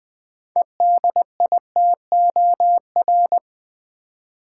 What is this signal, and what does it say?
Morse code, heard as a keyed tone, about 20 wpm, 700 Hz. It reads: EDITOR
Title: 6 Letter Words - First 3 Letters Given - 20wpm